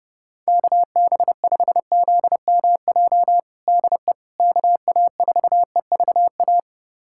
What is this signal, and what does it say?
Morse code, heard as a keyed tone, about 32 wpm, 700 Hz. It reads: KB5ZMJ DE KA4EVA
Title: USA Callsign DE Callsign - 6 Characters - 30wpm